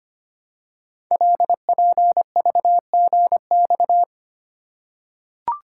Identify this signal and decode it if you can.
Morse code, heard as a keyed tone, about 25 wpm, 700 Hz. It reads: LPVGX
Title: Code Groups - 5 Letters - 25wpm